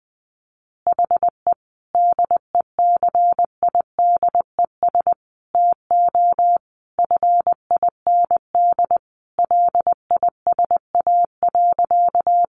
Morse code, heard as a keyed tone, about 20 wpm, 700 Hz. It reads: HE DECIDES TO FIND LISA.
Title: Additional Easy Phrases 20wpm